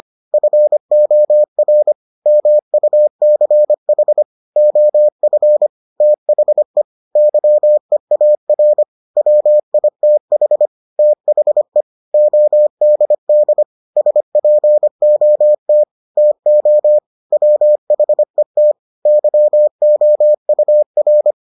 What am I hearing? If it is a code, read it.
Morse code, heard as a keyed tone, about 25 wpm, 600 Hz. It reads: FOR MUCH OF THE YEAR WITH THE ODD SPOT TO WHET YOUR